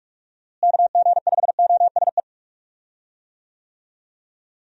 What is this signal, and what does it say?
Morse code, heard as a keyed tone, about 46 wpm, 700 Hz. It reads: KG5OSE
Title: USA Call Signs - 6 Characters - Repeated 2x - 45wpm